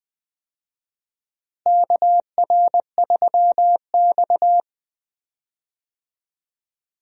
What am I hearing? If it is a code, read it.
Morse code, heard as a keyed tone, about 20 wpm, 700 Hz. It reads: KR3X
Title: USA Call Signs - 4 Characters - Repeated 2x - 20wpm